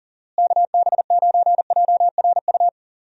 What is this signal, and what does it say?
Morse code, heard as a keyed tone, about 40 wpm, 700 Hz. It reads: KB9JRU
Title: USA Call Signs - 6 Characters - Repeated 2x - 40wpm